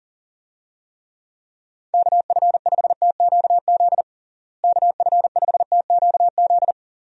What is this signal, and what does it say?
Morse code, heard as a keyed tone, about 40 wpm, 700 Hz. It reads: KF5TQZ KF5TQZ
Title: USA Call Signs - 6 Characters - Repeated 2x - 40wpm